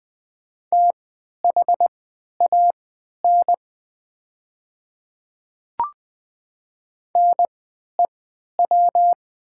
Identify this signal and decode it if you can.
Morse code, heard as a keyed tone, about 20 wpm, spaced as for 11 wpm, 700 Hz. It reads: THAN NEW
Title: 3x Character Spacing - Top 300 Words - 20wpm